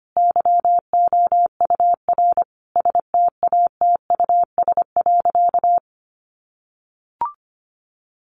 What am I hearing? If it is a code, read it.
Morse code, heard as a keyed tone, about 25 wpm, 700 Hz. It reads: YOUR STATUS.
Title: Sentences from Top 1400 Words 25wpm